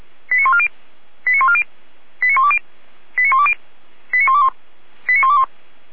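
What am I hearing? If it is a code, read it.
Five-tone selective call (ZVEI), heard as quick runs of five tones, tones of about 80 ms. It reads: 89140 89140 89120 89120 89121 89121
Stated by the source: ZVEI five-tone selective call received on SDR.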